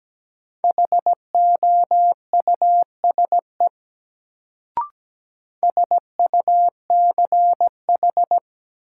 Morse code, heard as a keyed tone, about 17 wpm, 700 Hz. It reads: HOUSE SUCH
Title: Sentences from Top 200 Words - New Word Drill - 17wpm